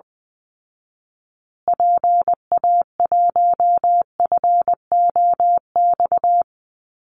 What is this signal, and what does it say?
Morse code, heard as a keyed tone, about 19 wpm, 700 Hz. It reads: PA1FOX
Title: Call Signs Repeated 2x - 20wpm